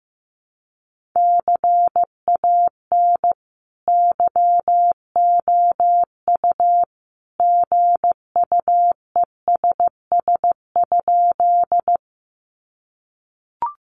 Morse code, heard as a keyed tone, about 15 wpm, 700 Hz. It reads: CAN YOU GUESS?
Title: Sentences from Top 1500 Words 15wpm